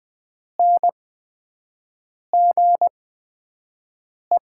Morse code, heard as a keyed tone, about 20 wpm, spaced as for 5 wpm, 700 Hz. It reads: NGE
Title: 8x Character Spacing - Top 200 Words - 20wpm